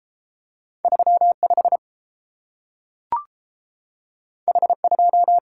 Morse code, heard as a keyed tone, about 30 wpm, 700 Hz. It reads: 3 5 H 2